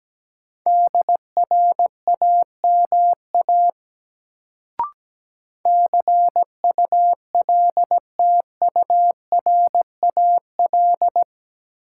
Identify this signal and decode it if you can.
Morse code, heard as a keyed tone, about 17 wpm, 700 Hz. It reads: DRAMA CULTURAL
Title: Sentences from Top 1500 Words - New Word Drill - 17wpm